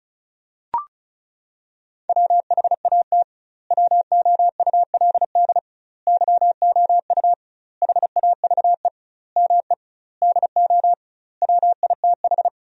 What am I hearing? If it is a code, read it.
Morse code, heard as a keyed tone, about 36 wpm, 700 Hz. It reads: WHAT WOULD YOU HAVE ME DO WITH
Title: 2x Word-Spacing - Sentences from Top 300 Words - 35wpm